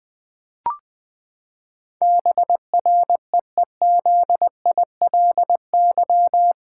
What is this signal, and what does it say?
Morse code, heard as a keyed tone, about 20 wpm, 700 Hz. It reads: BREEZILY
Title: Pangram Challenge v2 - 20wpm